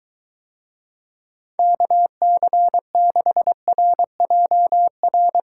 Morse code, heard as a keyed tone, about 23 wpm, 700 Hz. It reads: KC6RJR